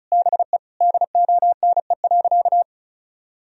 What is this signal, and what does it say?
Morse code, heard as a keyed tone, about 34 wpm, 700 Hz. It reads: BE DONE.
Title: Sentences from Top 1000 Words - Review Part 1 - 35wpm